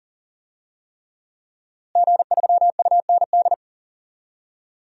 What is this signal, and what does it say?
Morse code, heard as a keyed tone, about 40 wpm, 700 Hz. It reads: G3UND
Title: DXCC-100 Call Signs - 5 Characters - Repeated 3x - 40wpm